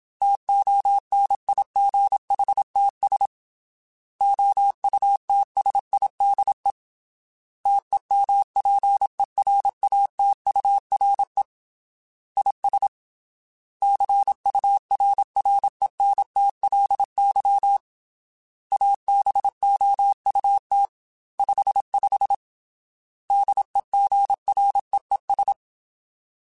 Morse code, heard as a keyed tone, about 27 wpm, 800 Hz. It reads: TONIGHTS OUTSIDE TEMPERATURE IS CURRENTLY ABOUT 55 DEGREES
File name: Weather Sets / Weather Sets-mp3s / Weather 4 24 @25